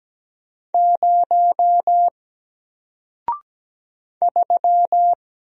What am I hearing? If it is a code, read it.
Morse code, heard as a keyed tone, about 17 wpm, 700 Hz. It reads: 0 3
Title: Single Letter-Number Alt 17wpm